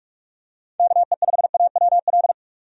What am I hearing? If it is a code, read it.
Morse code, heard as a keyed tone, about 42 wpm, 700 Hz. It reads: KE5AWL
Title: USA Call Signs - 6 Characters - 45wpm